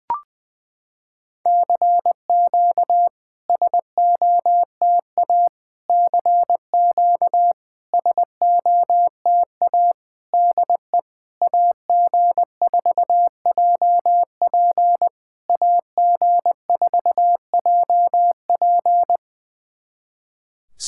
Morse code, heard as a keyed tone, about 21 wpm, 700 Hz. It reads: CQ SOTA CQ SOTA DE AG4JP AG4JP
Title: SOTA QSO Practice 20wpm